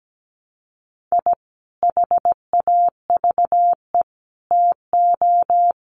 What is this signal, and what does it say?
Morse code, heard as a keyed tone, about 17 wpm, 700 Hz. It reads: I HAVE TO
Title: Sets of 5 Words 17wpm